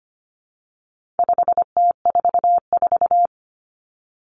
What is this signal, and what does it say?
Morse code, heard as a keyed tone, about 25 wpm, 700 Hz. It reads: 5T44
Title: Common Cut Numbers - 4-Digit 25wpm